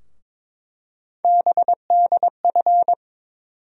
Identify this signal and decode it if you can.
Morse code, heard as a keyed tone, about 22 wpm, 700 Hz. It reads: BDF
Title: USA Call Sign Postfixes - 3 Letters 22wpm